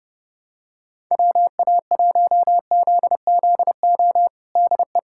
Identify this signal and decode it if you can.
Morse code, heard as a keyed tone, about 31 wpm, 700 Hz. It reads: WA1ZZO DE
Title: USA Callsign DE Callsign - 6 Characters - 30wpm